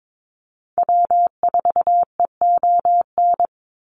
Morse code, heard as a keyed tone, about 22 wpm, 700 Hz. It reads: W4EON
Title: NA SOTA Activator Callsigns 22wpm